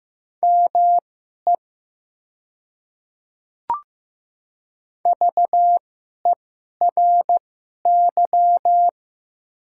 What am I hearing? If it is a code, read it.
Morse code, heard as a keyed tone, about 15 wpm, spaced as for 11 wpm, 700 Hz. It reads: ME VERY
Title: 2x Character Spacing - Top 500 Words - 15wpm